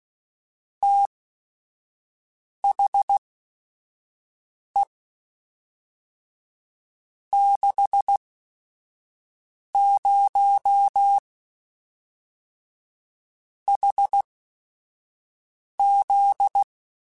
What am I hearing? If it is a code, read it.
Morse code, heard as a keyed tone, about 16 wpm, spaced as for 5 wpm, 800 Hz. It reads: THE 60 HZ
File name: Technical Subjects / Technical Subjects-mp3s / Sw Pwr Sup @5